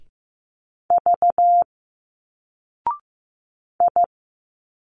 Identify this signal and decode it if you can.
Morse code, heard as a keyed tone, about 15 wpm, 700 Hz. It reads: V I
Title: Single Letters 15wpm